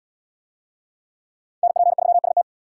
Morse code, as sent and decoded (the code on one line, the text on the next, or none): -. .--. ...-- ... -
NP3ST